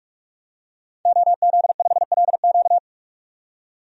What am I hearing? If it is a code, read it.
Morse code, heard as a keyed tone, about 42 wpm, 700 Hz. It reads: OZ5LQ
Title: DXCC-100 Call Signs - 5 Characters - 45wpm